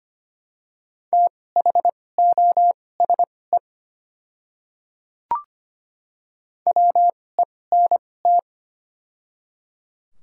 Morse code, heard as a keyed tone, about 25 wpm, spaced as for 18 wpm, 700 Hz. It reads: THOSE WENT
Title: 2x Character Spacing - Top 500 Words - 25wpm